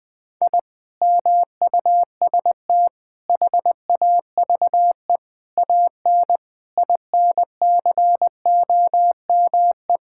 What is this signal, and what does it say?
Morse code, heard as a keyed tone, about 21 wpm, 700 Hz. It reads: I MUST HAVE AN INCOME
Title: Sentences from Top 1000 Words - Review Part 5 - 20wpm